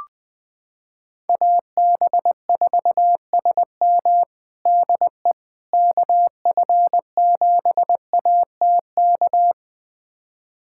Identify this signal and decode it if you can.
Morse code, heard as a keyed tone, about 20 wpm, 700 Hz. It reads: AB4SM DE KF7ATK
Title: USA Callsign DE Callsign 20wpm